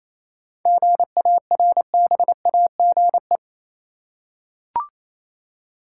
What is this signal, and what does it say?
Morse code, heard as a keyed tone, about 28 wpm, 700 Hz. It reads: GARBAGE
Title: Seven Letter Words - 28wpm